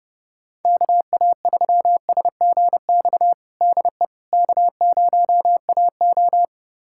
Morse code, heard as a keyed tone, about 30 wpm, 700 Hz. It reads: KA3SGX DE K0AO
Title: USA Callsign DE Callsign 30wpm